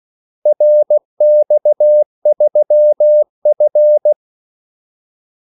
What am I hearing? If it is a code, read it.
Morse code, heard as a keyed tone, about 16 wpm, 600 Hz. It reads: RX3F
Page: Kaldesignaler 16 wpm